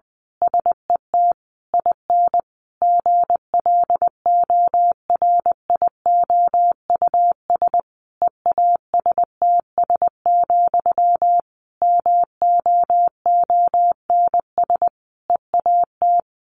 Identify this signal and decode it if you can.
Morse code, heard as a keyed tone, about 20 wpm, 700 Hz. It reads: SET IN GLORIOUS EASTS, MOONS EAT
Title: A Sound Of Thunder 20wpm